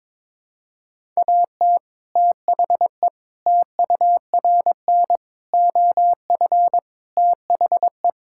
Morse code, heard as a keyed tone, about 22 wpm, 700 Hz. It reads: AT THE TURN OF THE
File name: practice / 22wpm / Sentences with 4-Word Phrases - 22wpm